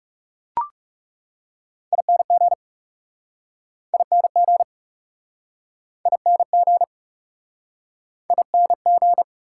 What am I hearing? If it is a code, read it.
Morse code, heard as a keyed tone, about 38 wpm, 700 Hz. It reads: ING ING ING ING